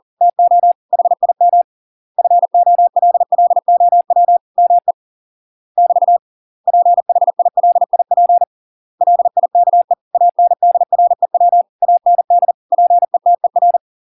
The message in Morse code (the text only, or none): TO HIM FOLLOW ME <BT> PHILIP LIKE ANDREW AND PETER